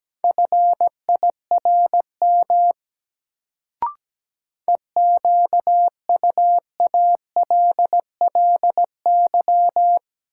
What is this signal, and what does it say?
Morse code, heard as a keyed tone, about 17 wpm, 700 Hz. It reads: FIRM EQUALLY
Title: Sentences from Top 1400 Words - New Word Drill Alt - 17wpm